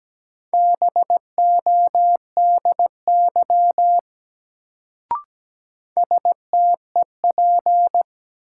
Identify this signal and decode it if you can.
Morse code, heard as a keyed tone, about 17 wpm, 700 Hz. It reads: BODY STEP
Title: Sentences from Top 400 Words - New Word Drill - 17wpm